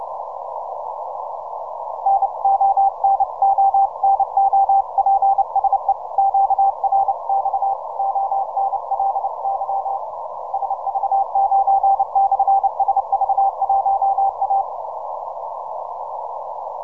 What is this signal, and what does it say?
Morse code, heard as a keyed tone, about 28 wpm, 790 Hz. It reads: NO NO NO PSE QRX ONLY SV9/SV1A
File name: simple-file-list / TL8TT 10MCW